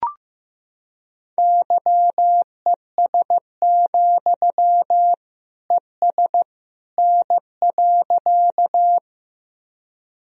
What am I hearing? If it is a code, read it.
Morse code, heard as a keyed tone, about 15 wpm, 700 Hz. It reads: YES, ES N.
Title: Sentences from Top 500 Words and Abbreviations 15wpm